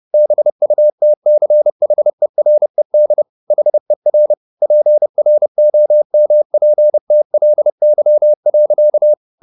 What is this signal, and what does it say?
Morse code, heard as a keyed tone, about 30 wpm, 600 Hz. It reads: BUTCHERED HER PROMPTLY.
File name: the old man and the sea / the old man and the sea 1 30wpm